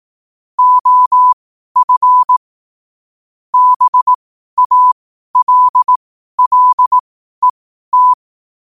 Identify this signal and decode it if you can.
Morse code, heard as a keyed tone, about 18 wpm, spaced as for 12 wpm, 1000 Hz. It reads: OF BALLET
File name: qotdfiles / 12wpm